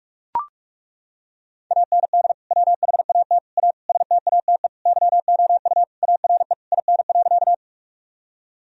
Morse code, heard as a keyed tone, about 41 wpm, 700 Hz. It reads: AND WHAT A STATE YOU ARE IN.
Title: Sentences from Top 1000 Words - Review Part 2 - 45wpm